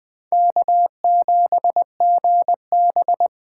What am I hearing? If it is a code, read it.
Morse code, heard as a keyed tone, about 20 wpm, 700 Hz. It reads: K7GB
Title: Speed Racing - USA 1x2 Call Signs 17wpm